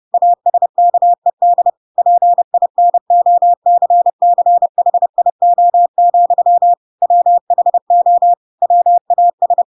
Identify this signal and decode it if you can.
Morse code, heard as a keyed tone, about 30 wpm, 700 Hz. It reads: ASKED PINOCCHIO, WHO WAS
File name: Pinocchio / Pinocchio ch4 30 30 ik1hge